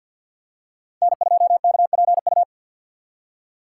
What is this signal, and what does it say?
Morse code, heard as a keyed tone, about 52 wpm, 700 Hz. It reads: N2KPU
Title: USA Call Signs - 5 Characters - 4x Character Spacing - 50wpm